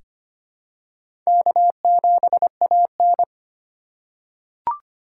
Morse code, heard as a keyed tone, about 25 wpm, 700 Hz. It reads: K7AN
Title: USA Call Signs - 4 Characters - 3x Character Spacing - 25wpm